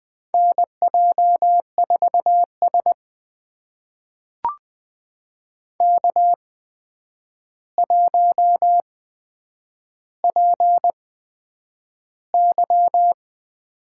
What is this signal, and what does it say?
Morse code, heard as a keyed tone, about 20 wpm, spaced as for 9 wpm, 700 Hz. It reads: NJ4S K1PY
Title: USA Call Signs - 4 Characters - 8x Character Spacing - 20wpm